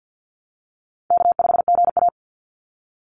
Morse code, heard as a keyed tone, about 50 wpm, 700 Hz. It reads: K5GA
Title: Call Signs - 50wpm